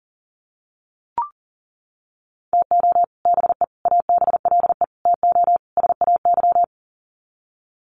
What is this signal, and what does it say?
Morse code, heard as a keyed tone, about 40 wpm, 700 Hz. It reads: TO BE ABLE TO SAY
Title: Sets of 5 Words Encore2 40wpm